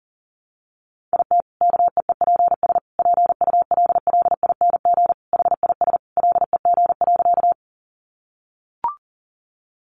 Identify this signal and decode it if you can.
Morse code, heard as a keyed tone, about 40 wpm, 700 Hz. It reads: IT KEEPS PULLING HIS LEG.
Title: Additional Easy Phrases 40wpm